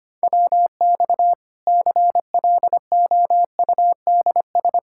Morse code, heard as a keyed tone, about 25 wpm, 700 Hz. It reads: WX CLOUDS
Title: Practice QSO - Exchange 2 - Info - 25wpm